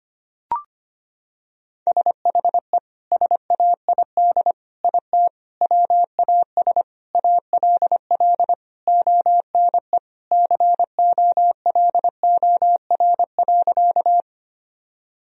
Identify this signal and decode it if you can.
Morse code, heard as a keyed tone, about 25 wpm, 700 Hz. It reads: SHE SAID IT WAS ALL ONE COLOR.
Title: Sentences from Top 1000 Words - Review Part 2 - 25wpm